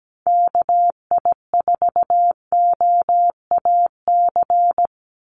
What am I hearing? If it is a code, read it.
Morse code, heard as a keyed tone, about 17 wpm, 700 Hz. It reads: KI4OAC
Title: USA Call Signs 17wpm